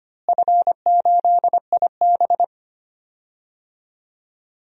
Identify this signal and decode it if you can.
Morse code, heard as a keyed tone, about 25 wpm, 700 Hz. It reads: F8IB
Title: DXCC-100 Call Signs - 4 Characters - Repeated 3x - 25wpm